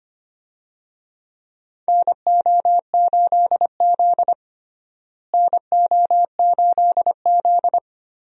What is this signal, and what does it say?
Morse code, heard as a keyed tone, about 25 wpm, 700 Hz. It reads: NO8Z NO8Z
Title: USA Call Signs - 4 Characters - Repeated 2x - 25wpm